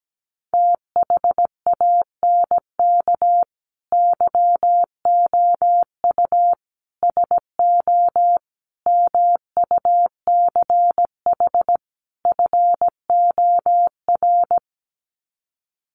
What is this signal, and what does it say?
Morse code, heard as a keyed tone, about 17 wpm, 700 Hz. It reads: THANK YOU SO MUCH FOR
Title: Sets of 5 Words 17wpm